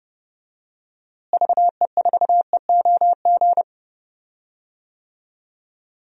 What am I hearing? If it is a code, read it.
Morse code, heard as a keyed tone, about 30 wpm, 700 Hz. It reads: VE4EOG
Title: DXCC-100 Call Signs - 6 Characters - Repeated 3x - 30wpm